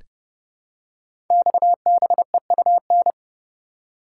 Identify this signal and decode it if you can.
Morse code, heard as a keyed tone, about 30 wpm, 700 Hz. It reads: XBEUN
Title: Code Groups - 5 Letters - 30wpm